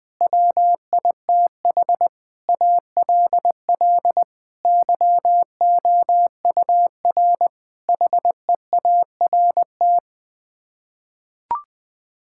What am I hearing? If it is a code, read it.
Morse code, heard as a keyed tone, about 20 wpm, 700 Hz. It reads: WITH ALL YOUR HEART
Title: Sets of 9 Words - 20wpm